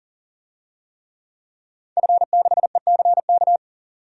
Morse code, heard as a keyed tone, about 40 wpm, 700 Hz. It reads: F6ECK
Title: DXCC-100 Call Signs - 5 Characters - 40wpm